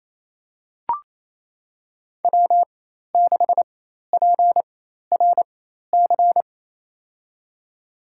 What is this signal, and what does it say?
Morse code, heard as a keyed tone, about 27 wpm, spaced as for 13 wpm, 700 Hz. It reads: W6PRC
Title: USA Call Signs - 4x Character Spacing - 28wpm